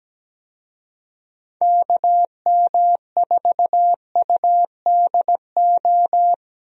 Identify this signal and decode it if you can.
Morse code, heard as a keyed tone, about 17 wpm, 700 Hz. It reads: KM4UDO
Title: USA Call Signs - 6 Characters - Repeated 2x - 17wpm